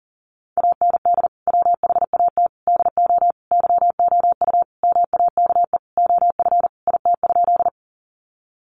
Morse code, heard as a keyed tone, about 40 wpm, 700 Hz. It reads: AND WHAT DO YOU MAKE OF IT?
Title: Sentences from Top 100 Words 40wpm